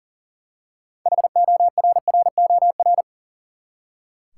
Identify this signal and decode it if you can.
Morse code, heard as a keyed tone, about 38 wpm, 700 Hz. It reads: HORROR
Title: Six Letter Words - 40wpm